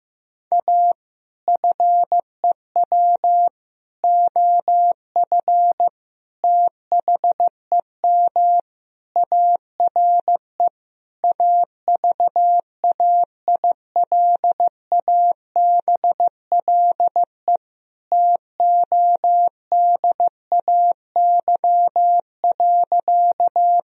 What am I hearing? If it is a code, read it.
Morse code, heard as a keyed tone, about 16 wpm, 700 Hz. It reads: A FEW OF THEM ARE AVAILABLE TODAY.
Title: Sentences from Top 1000 Words - Review Part 3 - 15wpm